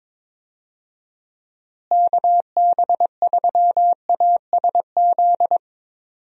Morse code, heard as a keyed tone, about 22 wpm, 700 Hz. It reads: KB3ASZ
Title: USA Call Signs - 6 Characters - Repeated 2x - 22wpm